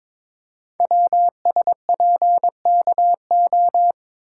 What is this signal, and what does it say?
Morse code, heard as a keyed tone, about 22 wpm, 700 Hz. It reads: WSPKO
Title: Code Groups - 5 Letters - 22wpm